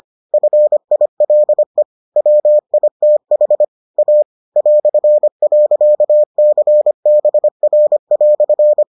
Morse code, heard as a keyed tone, about 25 wpm, 600 Hz. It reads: FILE WITH A ".CBR"